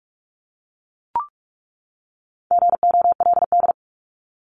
Morse code, heard as a keyed tone, about 44 wpm, 700 Hz. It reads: GOLD
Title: Top 500 Words 45wpm